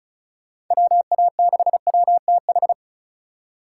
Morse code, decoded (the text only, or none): WA6WTH